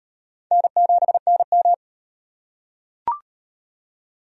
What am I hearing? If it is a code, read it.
Morse code, heard as a keyed tone, about 37 wpm, 700 Hz. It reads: N7NM